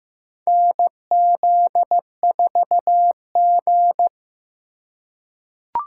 Morse code, heard as a keyed tone, about 15 wpm, 700 Hz. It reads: NZ4G
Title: USA Call Signs - 4 Characters - 3x Character Spacing - 15wpm